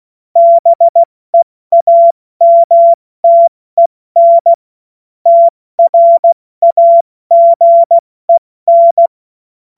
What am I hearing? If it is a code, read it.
Morse code, heard as a keyed tone, about 16 wpm, 680 Hz. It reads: BEAMTEN TRAGEN
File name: Witz-07-Oktober-2025-70-F80